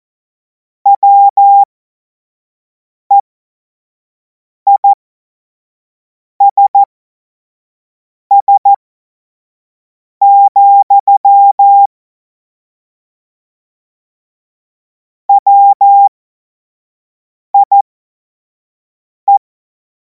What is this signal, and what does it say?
Morse code, heard as a keyed tone, about 14 wpm, spaced as for 5 wpm, 800 Hz. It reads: WEISS, WIE
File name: Morsekurs / Witze / Witz 05. Mai 2026 25 F70